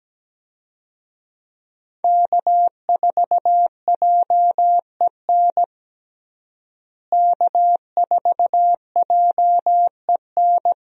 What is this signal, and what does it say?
Morse code, heard as a keyed tone, about 17 wpm, 700 Hz. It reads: K4JEN K4JEN
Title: USA Call Signs - 5 Characters - Repeated 3x - 17wpm